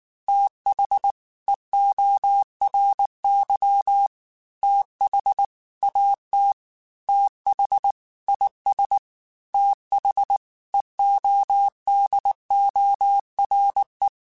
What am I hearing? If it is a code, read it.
Morse code, heard as a keyed tone, about 19 wpm, 800 Hz. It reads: TH EORY TH AT TH IS TH EODORE